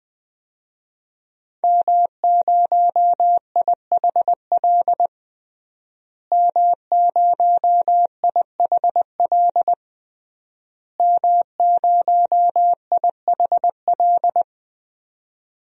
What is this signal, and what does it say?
Morse code, heard as a keyed tone, about 20 wpm, 700 Hz. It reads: M0IHL M0IHL M0IHL
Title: DXCC-100 Call Signs - 5 Characters - Repeated 3x - 20wpm